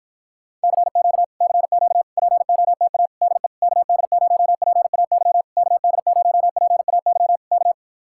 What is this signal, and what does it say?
Morse code, heard as a keyed tone, about 53 wpm, 700 Hz. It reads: CQ CQ POTA DE KD0PAY KD0PAY K